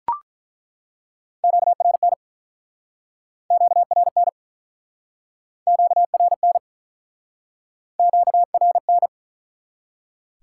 Morse code, heard as a keyed tone, about 48 wpm, 700 Hz. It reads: QRN QRN QRN QRN